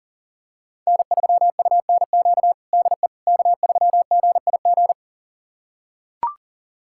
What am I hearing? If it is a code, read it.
Morse code, heard as a keyed tone, about 40 wpm, 700 Hz. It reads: N3UNQ DE K3GIG
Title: USA Callsign DE Callsign - 5 Characters - 40wpm